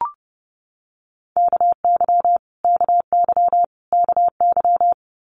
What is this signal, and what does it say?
Morse code, heard as a keyed tone, about 31 wpm, 700 Hz. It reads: KY KY KY
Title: Multi-letter Phonemes Repeated 3x 30wpm